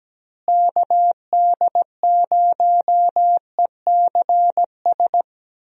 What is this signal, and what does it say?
Morse code, heard as a keyed tone, about 17 wpm, 700 Hz. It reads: KD0ECS
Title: USA Call Signs - 6 Characters - Repeated 3x - 17wpm